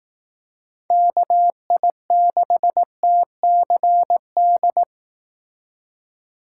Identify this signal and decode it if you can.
Morse code, heard as a keyed tone, about 18 wpm, 700 Hz. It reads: KI6TCD
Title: Speed Racing - USA Call Signs 15wpm